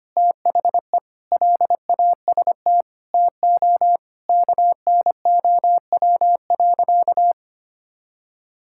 Morse code, heard as a keyed tone, about 25 wpm, 700 Hz. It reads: THE LAST TO KNOW.
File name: practice / 25wpm / Sentences with 2-Word Phrases - 25wpm